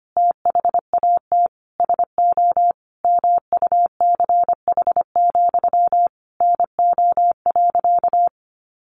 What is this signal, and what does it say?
Morse code, heard as a keyed tone, about 25 wpm, 700 Hz. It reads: THAT SO MUCH, NO.
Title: A Sound Of Thunder 25wpm